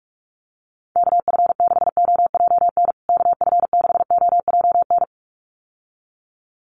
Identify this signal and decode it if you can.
Morse code, heard as a keyed tone, about 44 wpm, 700 Hz. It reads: KF6OJN KF6OJN
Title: USA Call Signs Repeated 2x 45wpm